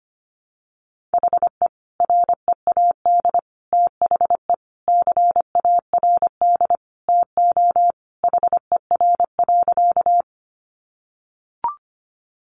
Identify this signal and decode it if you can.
Morse code, heard as a keyed tone, about 25 wpm, 700 Hz. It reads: HE READ THE CARD TO HER.
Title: Sentences from Top 600 Words 25wpm